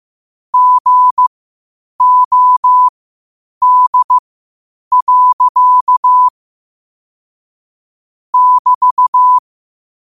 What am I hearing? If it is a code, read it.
Morse code, heard as a keyed tone, about 15 wpm, spaced as for 8 wpm, 1000 Hz. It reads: GOD. <BT>
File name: qotdfiles / 8wpm